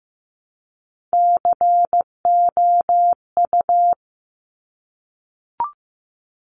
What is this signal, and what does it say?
Morse code, heard as a keyed tone, about 15 wpm, 700 Hz. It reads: COU
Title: Top Three Letter Combinations in Words 15wpm